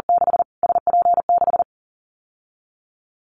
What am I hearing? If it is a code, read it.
Morse code, heard as a keyed tone, about 40 wpm, 700 Hz. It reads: 6 SP6
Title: Callsign Prefixes for Top 25 DXCC - Repeated 3x - 40wpm